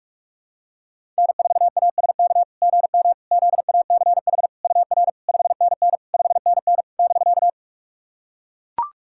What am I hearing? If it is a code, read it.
Morse code, heard as a keyed tone, about 44 wpm, 690 Hz. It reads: N4ASK GM ZACH UR 5NN 5NN <BK>
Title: SOTA QSO Practice 45wpm